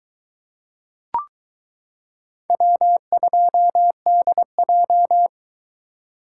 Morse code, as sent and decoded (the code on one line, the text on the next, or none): .-- ..--- -.. .---
W2DJ